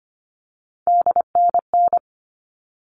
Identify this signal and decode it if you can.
Morse code, heard as a keyed tone, about 25 wpm, 700 Hz. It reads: DNN
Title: All Cut Numbers - 3-Digit 25wpm